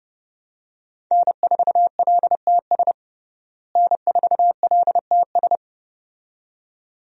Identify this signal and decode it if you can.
Morse code, heard as a keyed tone, about 32 wpm, 700 Hz. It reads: N4LTS N4LTS
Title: USA Call Signs - 5 Characters - Repeated 2x - 30wpm